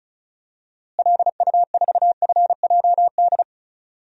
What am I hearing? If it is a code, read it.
Morse code, heard as a keyed tone, about 36 wpm, 700 Hz. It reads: LU4FJD
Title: DXCC-100 Call Signs - 6 Characters - Repeated 3x - 35wpm